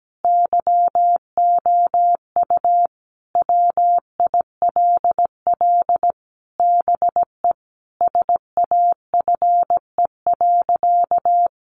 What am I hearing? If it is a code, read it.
Morse code, heard as a keyed tone, about 17 wpm, 700 Hz. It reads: YOU WILL BE SAFE.
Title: Sentences from Top 1300 Words 17wpm